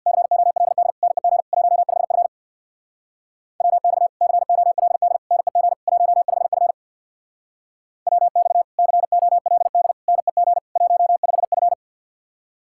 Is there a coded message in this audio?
Yes: Morse code from 0.1 to 11.7 s, heard as a keyed tone, about 64 wpm, 700 Hz. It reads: COLD NEG 15F WX COLD NEG 15F WX COLD NEG 15F